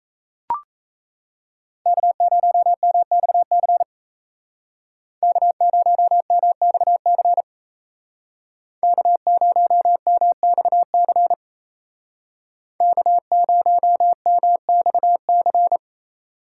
Morse code, heard as a keyed tone, about 34 wpm, 700 Hz. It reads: K0MXC K0MXC K0MXC K0MXC
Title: Speed Racing - USA 1x3 Call Signs 28wpm